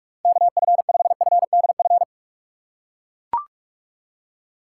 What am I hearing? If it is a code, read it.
Morse code, heard as a keyed tone, about 42 wpm, 700 Hz. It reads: KF5FDF
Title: USA Call Signs - 6 Characters - 3x Character Spacing - 45wpm